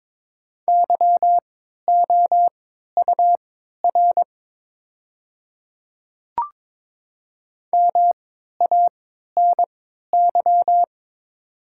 Morse code, heard as a keyed tone, about 22 wpm, spaced as for 12 wpm, 690 Hz. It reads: YOUR MANY
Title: 3x Character Spacing - Top 200 Words - 22wpm